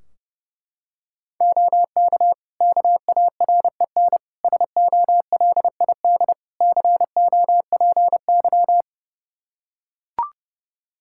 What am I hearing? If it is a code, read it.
Morse code, heard as a keyed tone, about 32 wpm, 700 Hz. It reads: OK KAREN SOLID COPY
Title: Practice QSO - Exchange 3 - More Info - 30wpm